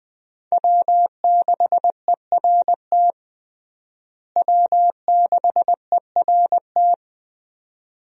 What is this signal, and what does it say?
Morse code, heard as a keyed tone, about 21 wpm, 700 Hz. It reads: W6ERT W6ERT
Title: USA Call Signs - 5 Characters - Repeated 2x - 20wpm